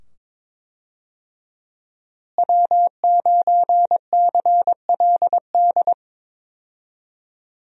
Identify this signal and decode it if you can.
Morse code, heard as a keyed tone, about 21 wpm, 700 Hz. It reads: W9CLD
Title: USA Call Signs - 5 Characters - Repeated 2x - 22wpm